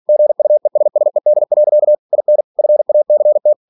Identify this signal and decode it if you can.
Morse code, heard as a keyed tone, about 46 wpm, 600 Hz. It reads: GUESSED. IN FACT
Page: Never 47wpm